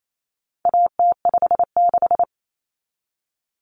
Morse code, heard as a keyed tone, about 28 wpm, 700 Hz. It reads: AT56
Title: Common Cut Numbers - 4-Digit 28wpm